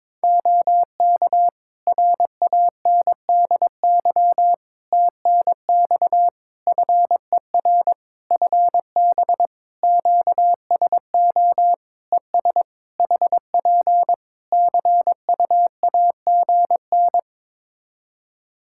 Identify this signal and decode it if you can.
Morse code, heard as a keyed tone, about 22 wpm, 700 Hz. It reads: OK RANDY TNX FER FB QSO ES HP CUAGN
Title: Practice QSO - Exchange 4 - Ending - 22wpm